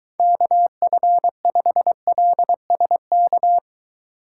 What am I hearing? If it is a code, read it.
Morse code, heard as a keyed tone, about 23 wpm, 700 Hz. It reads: KF5LSK